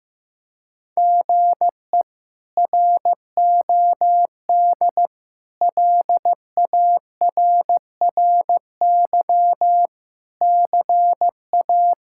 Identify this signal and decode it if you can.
Morse code, heard as a keyed tone, about 16 wpm, 700 Hz. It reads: GE ROD LARRY CA
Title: K1USN SST QSO Practice - 15wpm